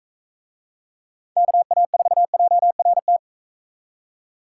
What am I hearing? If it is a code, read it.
Morse code, heard as a keyed tone, about 42 wpm, 700 Hz. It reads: KA4JRT